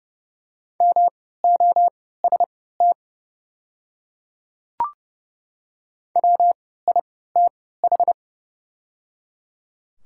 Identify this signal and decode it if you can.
Morse code, heard as a keyed tone, about 30 wpm, spaced as for 17 wpm, 700 Hz. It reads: MOST WITH
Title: 3x Character Spacing - Top 100 Words - 30wpm